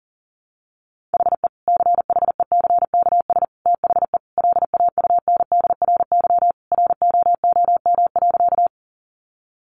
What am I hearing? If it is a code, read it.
Morse code, heard as a keyed tone, about 37 wpm, 700 Hz. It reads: HE CHECKS THE LAUNDRY ROOM.
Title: Easy Phrases 40wpm